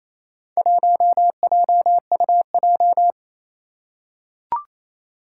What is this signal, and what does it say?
Morse code, heard as a keyed tone, about 27 wpm, 700 Hz. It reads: 1JUJ
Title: Four Challenging Letter-Number Combinations 28wpm